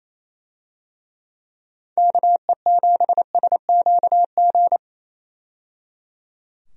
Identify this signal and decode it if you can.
Morse code, heard as a keyed tone, about 28 wpm, 700 Hz. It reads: KE7SQG
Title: USA Call Signs - 6 Characters - 28wpm